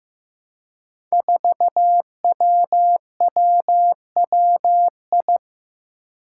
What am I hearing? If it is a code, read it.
Morse code, heard as a keyed tone, about 15 wpm, 700 Hz. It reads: 4WWWI
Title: Five Challenging Letter-Number Combinations 15wpm